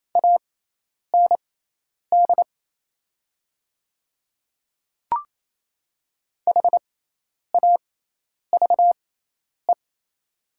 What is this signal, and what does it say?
Morse code, heard as a keyed tone, about 28 wpm, spaced as for 10 wpm, 700 Hz. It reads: AND HAVE
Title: 6x Character Spacing - Top 100 Words - 28wpm